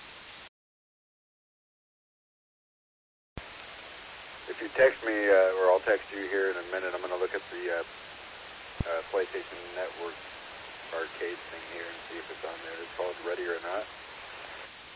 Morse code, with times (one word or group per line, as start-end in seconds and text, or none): none